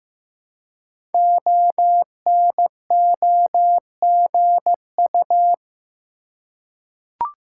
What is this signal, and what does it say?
Morse code, heard as a keyed tone, about 15 wpm, 700 Hz. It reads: ONOGU
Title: Code Groups - 5 Letters - 15wpm